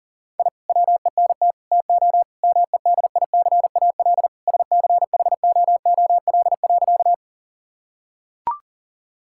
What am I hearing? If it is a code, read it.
Morse code, heard as a keyed tone, about 40 wpm, 700 Hz. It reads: I WENT TO MEDICAL SCHOOL.
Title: Sentences from Top 1000 Words - Review Part 5 - 40wpm